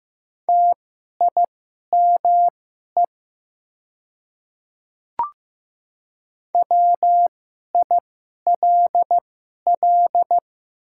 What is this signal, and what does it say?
Morse code, heard as a keyed tone, about 15 wpm, spaced as for 11 wpm, 700 Hz. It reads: TIME WILL
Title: 2x Character Spacing - Top 100 Words - 15wpm